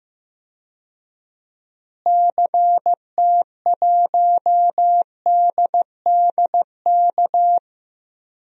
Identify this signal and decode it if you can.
Morse code, heard as a keyed tone, about 15 wpm, 700 Hz. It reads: CT1DDK
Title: DXCC-100 Call Signs - 6 Characters - Repeated 3x - 15wpm